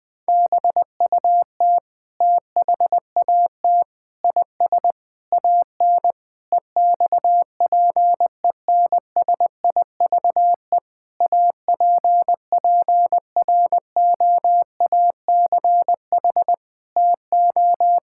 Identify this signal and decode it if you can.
Morse code, heard as a keyed tone, about 20 wpm, 700 Hz. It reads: BUT THAT IS AN EXPENSIVE APPROACH TO
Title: Sentences from Top 1000 Words - Review Part 6 - 20wpm